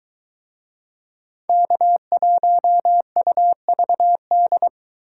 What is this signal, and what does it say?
Morse code, heard as a keyed tone, about 23 wpm, 700 Hz. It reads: K1UVD